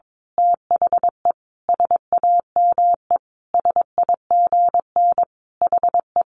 Morse code, heard as a keyed tone, about 22 wpm, 700 Hz. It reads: THE SAME SIGN HE
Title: A Sound Of Thunder 22wpm